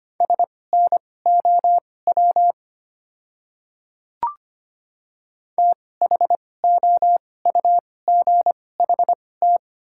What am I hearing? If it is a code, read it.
Morse code, heard as a keyed tone, about 25 wpm, spaced as for 18 wpm, 700 Hz. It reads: SNOW THOUGHT
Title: 2x Character Spacing - Top 500 Words - 25wpm